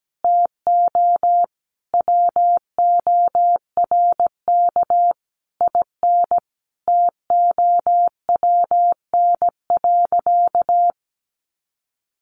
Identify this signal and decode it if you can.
Morse code, heard as a keyed tone, about 17 wpm, 700 Hz. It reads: TO WORK IN TOWN.
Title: Sentences from Top 1500 Words 17wpm